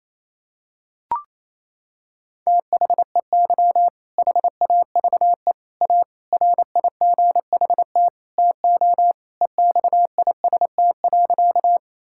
Morse code, heard as a keyed tone, about 28 wpm, 700 Hz. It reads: THEY HAVE A RIGHT TO EXIST.
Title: Sentences from Top 1000 Words - Review Part 5 - 28wpm